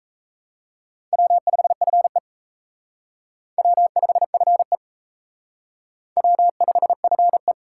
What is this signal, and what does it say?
Morse code, heard as a keyed tone, about 36 wpm, 700 Hz. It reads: W5FE W5FE W5FE